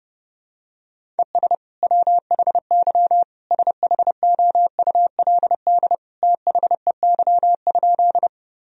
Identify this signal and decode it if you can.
Morse code, heard as a keyed tone, about 30 wpm, 700 Hz. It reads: ES WHY SHOULD THEY?
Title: Sentences from Top 300 Words and Abbreviations 30wpm